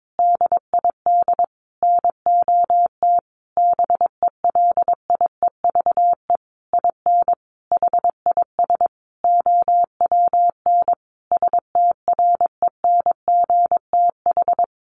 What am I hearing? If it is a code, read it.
Morse code, heard as a keyed tone, about 22 wpm, 700 Hz. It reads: DID NOT BELIEVE IN HIS OWN STRENGTH
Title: Sentences from Top 1300 Words 22wpm